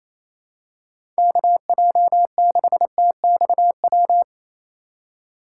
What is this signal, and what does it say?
Morse code, heard as a keyed tone, about 28 wpm, 700 Hz. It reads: KJ6TXW
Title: USA Call Signs - 6 Characters - 2x Character Spacing - 28wpm